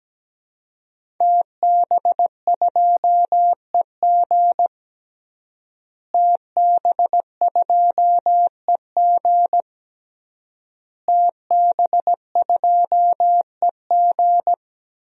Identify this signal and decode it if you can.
Morse code, heard as a keyed tone, about 17 wpm, 700 Hz. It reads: TB2EG TB2EG TB2EG
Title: DXCC-100 Call Signs - 5 Characters - Repeated 3x - 17wpm